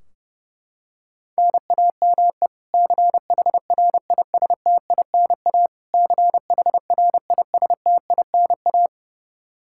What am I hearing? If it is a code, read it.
Morse code, heard as a keyed tone, about 32 wpm, 700 Hz. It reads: NAME CHRISTINA CHRISTINA
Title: K1USN SST QSO Practice 30wpm